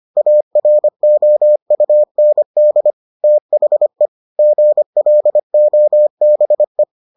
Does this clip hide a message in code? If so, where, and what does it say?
Yes: Morse code from 0.2 to 6.8 s, heard as a keyed tone, about 25 wpm, 600 Hz. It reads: AROUND THE GLOBE